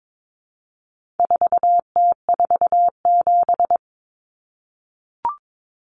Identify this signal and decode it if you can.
Morse code, heard as a keyed tone, about 22 wpm, 700 Hz. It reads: 4T47
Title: Common Cut Numbers - 4-Digit 22wpm